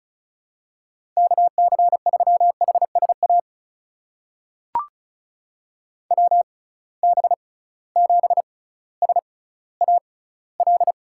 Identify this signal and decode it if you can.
Morse code, heard as a keyed tone, about 36 wpm, spaced as for 18 wpm, 700 Hz. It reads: KC3HSA WB7SAL
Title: USA Call Signs - 6 Characters - 6x Character Spacing - 35wpm